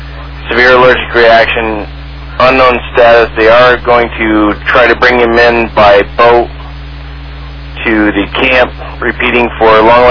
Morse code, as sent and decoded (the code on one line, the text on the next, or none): none